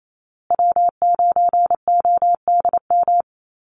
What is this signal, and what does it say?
Morse code, heard as a keyed tone, about 28 wpm, 700 Hz. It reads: W9ODM
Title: USA Call Signs Repeated 3x 28wpm